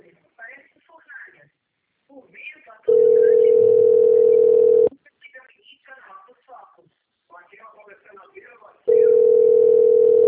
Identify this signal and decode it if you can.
Ringback tone (North America), 440 + 480 Hz, with speech.
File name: out- - - - - .4200